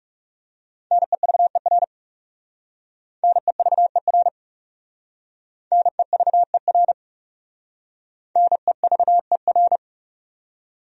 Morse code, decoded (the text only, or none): NEVER NEVER NEVER NEVER